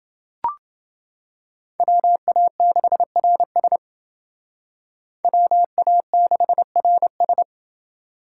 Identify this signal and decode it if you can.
Morse code, heard as a keyed tone, about 28 wpm, 700 Hz. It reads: WA6RS WA6RS